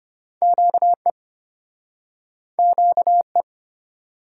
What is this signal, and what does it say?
Morse code, heard as a keyed tone, about 34 wpm, 700 Hz. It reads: QE QE